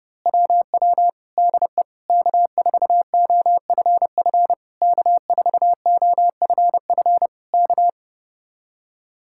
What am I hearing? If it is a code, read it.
Morse code, heard as a keyed tone, about 30 wpm, 700 Hz. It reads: WW DE K4OFF K4OFF K
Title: Speed-Racing Common QSO Phrases Encore - 30wpm